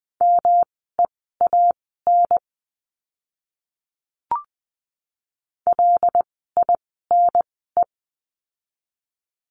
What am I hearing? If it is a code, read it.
Morse code, heard as a keyed tone, about 19 wpm, spaced as for 14 wpm, 700 Hz. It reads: MEAN LINE
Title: 108 Lesson 18f - Letter m - Words 20wpm